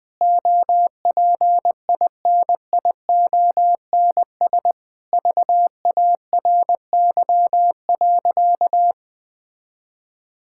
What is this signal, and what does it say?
Morse code, heard as a keyed tone, about 20 wpm, 700 Hz. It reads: OPINIONS VARY.
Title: Sentences with 7-Word Phrases - 20wpm